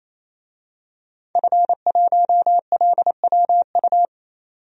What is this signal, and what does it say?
Morse code, heard as a keyed tone, about 28 wpm, 700 Hz. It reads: F1LWU
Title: DXCC-100 Call Signs - 5 Characters - 28wpm